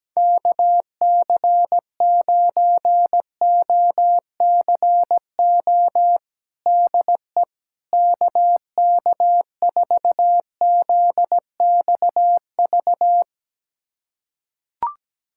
Morse code, heard as a keyed tone, about 17 wpm, 700 Hz. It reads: KC9OCO DE KK4ZXV
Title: USA Callsign DE Callsign 17wpm